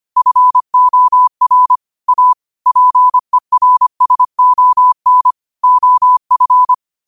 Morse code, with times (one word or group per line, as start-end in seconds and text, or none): 0.2-1.8 FOR
2.1-2.3 A
2.7-5.3 PERSON
5.6-6.7 OF